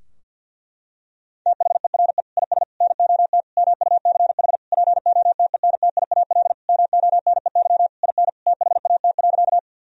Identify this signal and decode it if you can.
Morse code, heard as a keyed tone, about 50 wpm, 700 Hz. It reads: THERE IS NOT MUCH POTENTIAL MONEY IN THAT.
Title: Sentences from Top 1000 Words - Review Part 4 - 50wpm